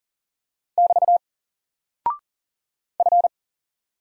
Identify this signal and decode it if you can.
Morse code, heard as a keyed tone, about 40 wpm, 700 Hz. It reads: <BT> F
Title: Single Character (Letter:Number:Punctuation:Prosign) 40wpm